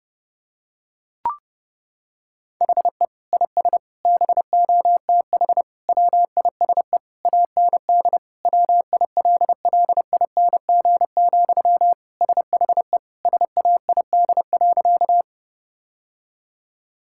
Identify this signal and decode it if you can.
Morse code, heard as a keyed tone, about 32 wpm, 700 Hz. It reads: HE IS BOTH WISE AND WILLING, SHE SAID.
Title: Sentences from Top 1600 Words 30wpm